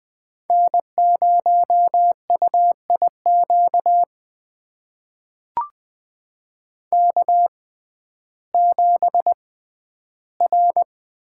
Morse code, heard as a keyed tone, about 20 wpm, spaced as for 12 wpm, 700 Hz. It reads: N0UIQ K7R
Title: USA Call Signs - 5 Characters - 6x Character Spacing - 20wpm